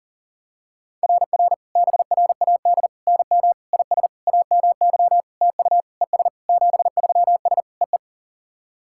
Morse code, heard as a keyed tone, about 44 wpm, 690 Hz. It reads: RR BRAD NM IS AMY TU ES 73S EE
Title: SOTA QSO Practice 40wpm